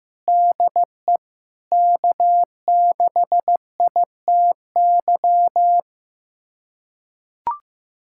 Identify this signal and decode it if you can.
Morse code, heard as a keyed tone, about 15 wpm, 700 Hz. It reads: DE K6ITY
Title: USA Callsign DE Callsign - 5 Characters - 15wpm